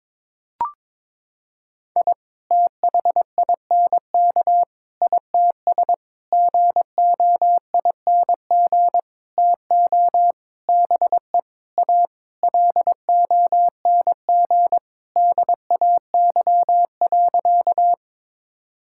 Morse code, heard as a keyed tone, about 22 wpm, 700 Hz. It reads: I THINK ITS GOING TO BE A LONG DAY.
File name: Sentences with 7-Word Phrases - 22wpm